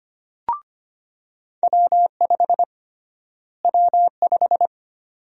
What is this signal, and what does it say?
Morse code, heard as a keyed tone, about 25 wpm, 700 Hz. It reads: W5 W5
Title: USA Call Sign Prefixes - Repeated 2x - 25wpm